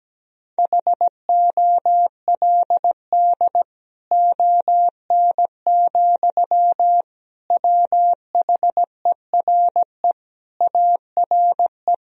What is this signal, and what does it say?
Morse code, heard as a keyed tone, about 17 wpm, 700 Hz. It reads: HOLD ON, WHERE ARE
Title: Sentences from Top 1000 Words - Review Part 2 - 17wpm